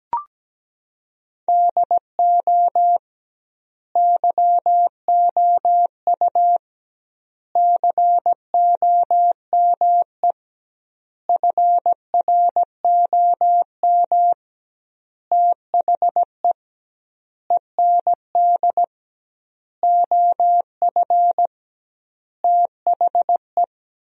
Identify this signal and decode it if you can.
Morse code, heard as a keyed tone, about 17 wpm, 700 Hz. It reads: DO YOU COME FROM THE END OF THE
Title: 2x Word-Spacing - Sentences from Top 200 Words - 17wpm